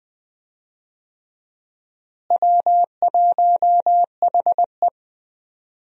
Morse code, heard as a keyed tone, about 20 wpm, 700 Hz. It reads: W1HE
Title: USA Call Signs - 4 Characters - Repeated 3x - 20wpm